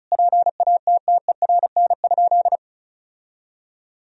Morse code, heard as a keyed tone, about 36 wpm, 700 Hz. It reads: PATTERN?
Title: Sentences from Top 1000 Words - Review Part 2 - 35wpm